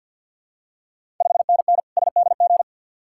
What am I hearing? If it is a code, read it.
Morse code, heard as a keyed tone, about 51 wpm, 690 Hz. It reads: 5NN SDG
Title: Ontario QSO Party Exchange 50wpm